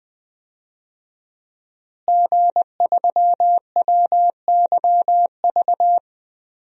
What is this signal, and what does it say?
Morse code, heard as a keyed tone, about 20 wpm, 700 Hz. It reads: G3WYV
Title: DXCC-100 Call Signs - 5 Characters - Repeated 3x - 20wpm